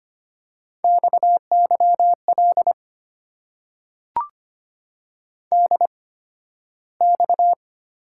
Morse code, heard as a keyed tone, about 25 wpm, spaced as for 13 wpm, 700 Hz. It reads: XYL DX
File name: Top 90 QSO Elements - 8x Character Spacing - 25wpm